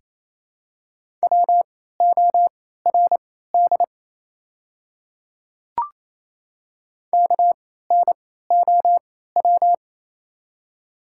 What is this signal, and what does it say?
Morse code, heard as a keyed tone, about 28 wpm, spaced as for 16 wpm, 700 Hz. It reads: WORD KNOW
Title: 3x Character Spacing - Top 200 Words - 28wpm